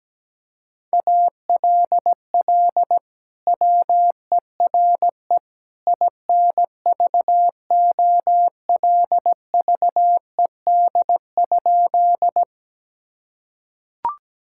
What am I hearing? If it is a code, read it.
Morse code, heard as a keyed tone, about 17 wpm, 700 Hz. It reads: ALL WERE INVOLVED?
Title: Sentences from Top 1000 Words - Review Part 5 - 17wpm